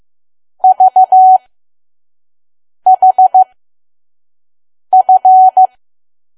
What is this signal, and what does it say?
Morse code, heard as a keyed tone, about 15 wpm, spaced as for 5 wpm, 750 Hz. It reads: VHF
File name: Archive / 5 WPM files / PM